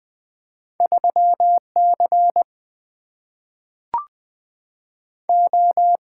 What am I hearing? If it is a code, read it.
Morse code, heard as a keyed tone, about 20 wpm, 700 Hz. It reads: 3C O
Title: Single-Letter Single-Number 20wpm